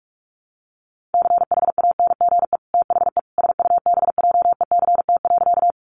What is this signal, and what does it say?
Morse code, heard as a keyed tone, about 44 wpm, 700 Hz. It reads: CHANGE THE SUBJECT.
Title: Sentences from Top 800 Words 45wpm